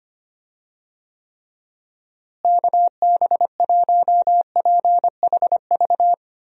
Morse code, heard as a keyed tone, about 25 wpm, 700 Hz. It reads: KB1PHV
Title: USA Call Signs - 6 Characters - Repeated 2x - 25wpm